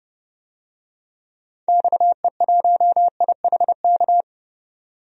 Morse code, heard as a keyed tone, about 30 wpm, 700 Hz. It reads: XE1IHK
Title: DXCC-100 Call Signs - 6 Characters - 30wpm